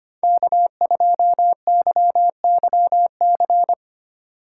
Morse code, heard as a keyed tone, about 31 wpm, 700 Hz. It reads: K2YYC
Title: Speed Racing - USA 1x3 Call Signs 25wpm